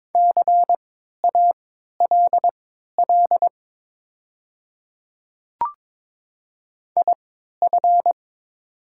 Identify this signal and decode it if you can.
Morse code, heard as a keyed tone, about 22 wpm, spaced as for 12 wpm, 700 Hz. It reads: CALL IF
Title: 3x Character Spacing - Top 100 Words - 22wpm